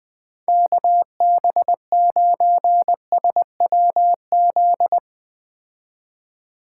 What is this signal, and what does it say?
Morse code, heard as a keyed tone, about 20 wpm, 700 Hz. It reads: KB9SWZ
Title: Speed Racing - USA Call Signs 17wpm